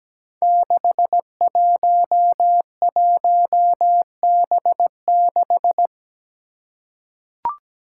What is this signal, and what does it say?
Morse code, heard as a keyed tone, about 17 wpm, 700 Hz. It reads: 611B6
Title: Five Challenging Letter-Number Combinations 17wpm